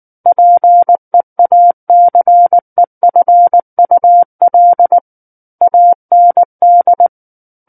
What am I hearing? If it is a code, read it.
Morse code, heard as a keyed tone, about 19 wpm, 700 Hz. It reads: PEACEFUL AND
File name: Book One / 10-Chapter 10 - 19 WPM